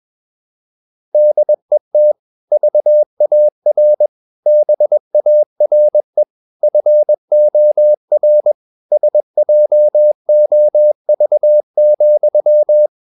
Morse code, heard as a keyed tone, about 21 wpm, 600 Hz. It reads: DET VAR BARE FOR SJOV,
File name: Ferie QRS 21wpm